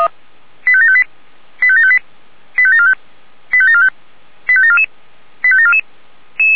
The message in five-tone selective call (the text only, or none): 85758 85758 85745 85745 85744 85744